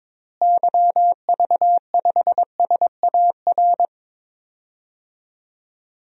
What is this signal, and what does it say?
Morse code, heard as a keyed tone, about 22 wpm, 700 Hz. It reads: YV5SAR
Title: DXCC-100 Call Signs - 6 Characters - Repeated 3x - 22wpm